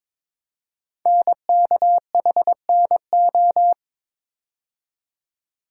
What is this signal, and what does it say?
Morse code, heard as a keyed tone, about 22 wpm, 700 Hz. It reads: NKHNO
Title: Code Groups - 5 Letters - 22wpm